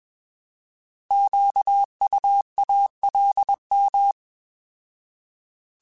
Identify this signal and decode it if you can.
Morse code, heard as a keyed tone, about 21 wpm, 800 Hz. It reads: QUALM
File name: Alphabetical / Alphabetical mp3s / Qs @20